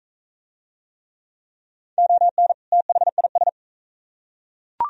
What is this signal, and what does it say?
Morse code, heard as a keyed tone, about 40 wpm, 700 Hz. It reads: ON THIS